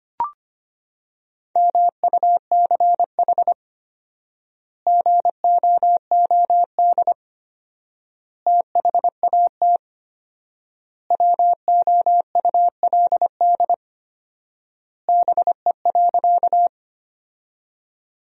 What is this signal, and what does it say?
Morse code, heard as a keyed tone, about 25 wpm, 700 Hz. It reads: MUCH GOOD THAT WOULD BE.
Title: 4x Word-Spacing - Sentences from Top 200 Words - 25wpm